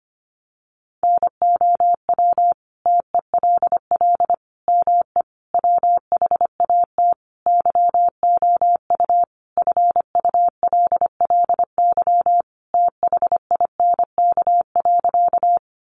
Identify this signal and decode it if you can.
Morse code, heard as a keyed tone, about 25 wpm, 700 Hz. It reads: NOW TELL ME WHAT YOU FULLY THINK.
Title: Sentences from Top 1300 Words 25wpm